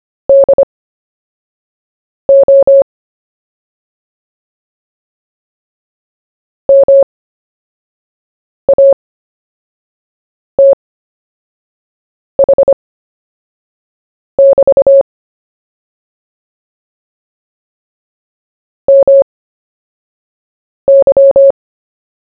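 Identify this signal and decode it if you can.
Morse code, heard as a keyed tone, about 25 wpm, spaced as for 5 wpm, 550 Hz. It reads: DO MATH<BT> MY
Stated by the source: CW story-telling practice, Short Story 1 sent at 5 wpm, by VK4JE, Queensland, Australia.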